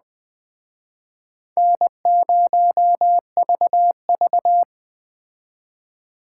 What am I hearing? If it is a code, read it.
Morse code, heard as a keyed tone, about 21 wpm, 700 Hz. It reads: N0VV
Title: Speed Racing - USA 1x2 Call Signs 17wpm